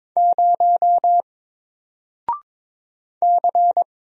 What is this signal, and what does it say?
Morse code, heard as a keyed tone, about 22 wpm, 700 Hz. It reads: 0 C
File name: Single-Letter-Number-22wpm